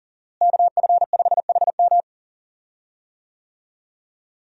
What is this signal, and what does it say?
Morse code, heard as a keyed tone, about 40 wpm, 700 Hz. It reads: KF5HM
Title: USA Call Signs - 5 Characters - Repeated 3x - 40wpm